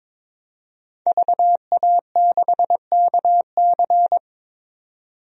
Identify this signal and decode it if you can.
Morse code, heard as a keyed tone, about 22 wpm, 700 Hz. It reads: VA6KC
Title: DXCC-100 Call Signs - 5 Characters - 22wpm